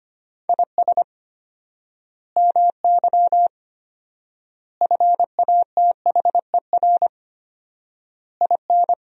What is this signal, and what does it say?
Morse code, heard as a keyed tone, about 25 wpm, 700 Hz. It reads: IS MY FATHER IN
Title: 4x Word-Spacing - Sentences from Top 500 Words - 25wpm